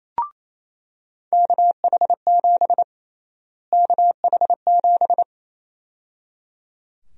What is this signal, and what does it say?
Morse code, heard as a keyed tone, about 28 wpm, 700 Hz. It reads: KH7 KH7
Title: USA Call Sign Prefixes - Repeated 2x - 28wpm